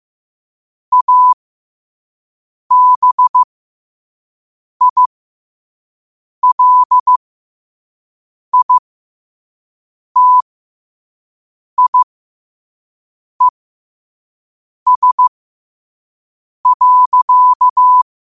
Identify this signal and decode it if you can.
Morse code, heard as a keyed tone, about 15 wpm, spaced as for 5 wpm, 1000 Hz. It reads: ABILITIES.
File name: qotdfiles / 5wpm